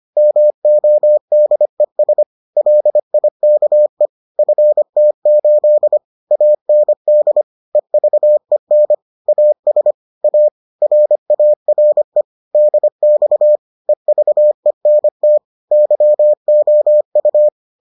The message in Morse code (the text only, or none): MODES LIKE FT8 AND EVEN AS A RARE DX EVENT YOU